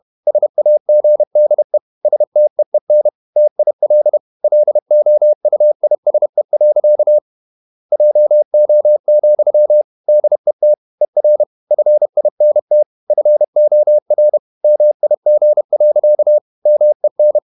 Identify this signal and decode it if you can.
Morse code, heard as a keyed tone, about 31 wpm, 600 Hz. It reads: SAGDE STEEN TIL LOUISE. JO, DET ER FINT FOR MIG. MEN
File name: Ferie QRQ 31wpm